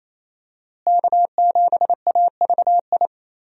Morse code, heard as a keyed tone, about 28 wpm, 700 Hz. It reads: K7AVI
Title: USA Call Signs - 5 Characters - Repeated 2x - 28wpm